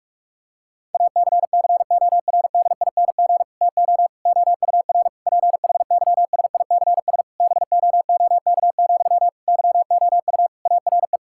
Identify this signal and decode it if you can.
Morse code, heard as a keyed tone, about 45 wpm, 690 Hz. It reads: ACCORDING TO OUR PHYSICS BOOK, YOU ARE
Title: Sentences from Top 1000 Words - Review Part 5 - 45wpm